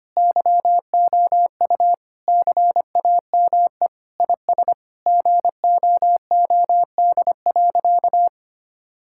Morse code, heard as a keyed tone, about 25 wpm, 700 Hz. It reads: YOU CAME IS GOOD.
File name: practice / 25wpm / Sentences with 3-Word Phrases - 25wpm